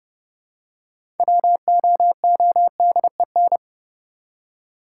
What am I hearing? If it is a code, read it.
Morse code, heard as a keyed tone, about 30 wpm, 700 Hz. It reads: WOODEN
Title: Six Letter Words - 30wpm